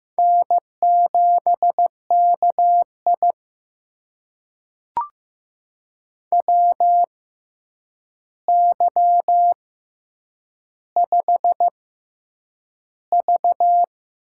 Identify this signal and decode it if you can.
Morse code, heard as a keyed tone, about 15 wpm, spaced as for 8 wpm, 700 Hz. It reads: N7KI WY5V
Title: USA Call Signs - 4 Characters - 6x Character Spacing - 15wpm